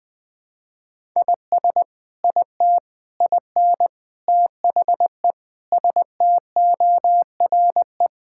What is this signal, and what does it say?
Morse code, heard as a keyed tone, about 21 wpm, 700 Hz. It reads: IS IT IN THE STORE
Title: Sentences from Top 1000 Words - Review Part 5 - 20wpm